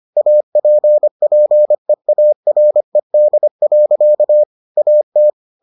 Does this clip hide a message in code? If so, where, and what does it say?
Yes: Morse code from 0.2 to 5.3 s, heard as a keyed tone, about 25 wpm, 600 Hz. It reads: APPEARED. AT